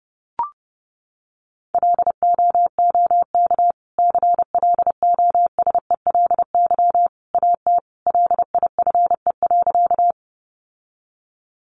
Morse code, heard as a keyed tone, about 30 wpm, 700 Hz. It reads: LOOK CLOSELY AT LIFE.
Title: Sentences from Top 1400 Words 30wpm